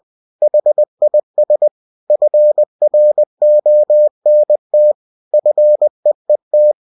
Morse code, heard as a keyed tone, about 20 wpm, 600 Hz. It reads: HIS FRONT FEET
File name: cw-books / winnie-the-pooh / winnie-20wpm-page-0026